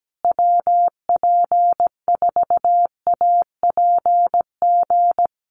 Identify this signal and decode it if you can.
Morse code, heard as a keyed tone, about 17 wpm, 700 Hz. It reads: WP4APG
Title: USA Call Signs Repeated 3x 17wpm